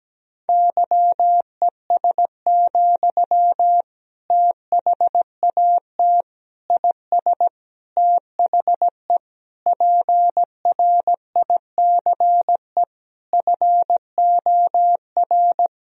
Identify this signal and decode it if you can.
Morse code, heard as a keyed tone, about 17 wpm, 700 Hz. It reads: YES, THAT IS THE PRICE FOR
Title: Sentences from Top 1000 Words - Review Part 4 - 17wpm